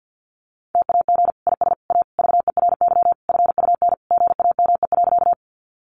Morse code, heard as a keyed tone, about 50 wpm, 700 Hz. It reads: TAG IS A VERY FUN GAME.
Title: Additional Easy Phrases 50wpm